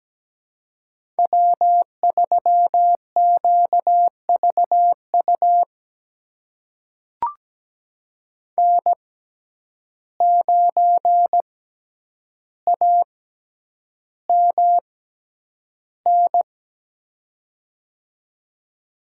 Morse code, heard as a keyed tone, about 17 wpm, spaced as for 9 wpm, 700 Hz. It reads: W3QVU N9AMN
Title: USA Call Signs - 5 Characters - 6x Character Spacing - 17wpm